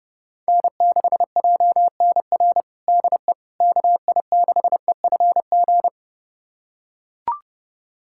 Morse code, heard as a keyed tone, about 30 wpm, 700 Hz. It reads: N6JNR DE KI6EFG
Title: USA Callsign DE Callsign 30wpm